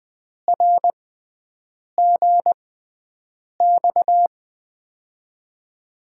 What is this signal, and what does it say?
Morse code, heard as a keyed tone, about 20 wpm, spaced as for 7 wpm, 700 Hz. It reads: RGX
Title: USA Call Signs - 6 Characters - 6x Character Spacing - 20wpm